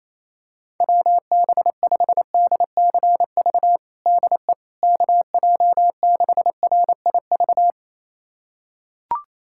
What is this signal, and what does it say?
Morse code, heard as a keyed tone, about 28 wpm, 700 Hz. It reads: WB5DCV DE KJ6RIV
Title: USA Callsign DE Callsign - 6 Characters - 28wpm